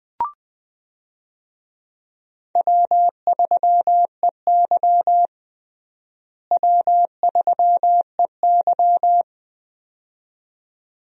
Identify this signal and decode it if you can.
Morse code, heard as a keyed tone, about 21 wpm, 700 Hz. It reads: W3EY W3EY
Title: USA Call Signs - 4 Characters - Repeated 2x - 20wpm